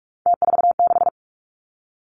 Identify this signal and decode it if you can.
Morse code, heard as a keyed tone, about 46 wpm, 700 Hz. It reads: T46
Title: Three Challenging Letter-Number Combinations 45wpm